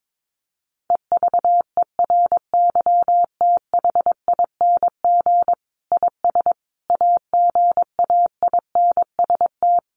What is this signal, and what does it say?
Morse code, heard as a keyed tone, about 22 wpm, 700 Hz. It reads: EVERYTHING IS AGAINST
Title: Sentences from Top 700 Words 22wpm